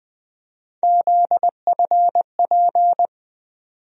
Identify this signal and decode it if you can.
Morse code, heard as a keyed tone, about 20 wpm, 700 Hz. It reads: ZFP
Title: USA Call Sign Postfixes - 3 Letters 20wpm